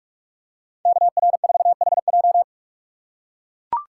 Morse code, heard as a keyed tone, about 44 wpm, 700 Hz. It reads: KR4HJ
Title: USA Call Signs - 5 Characters - 6x Character Spacing - 45wpm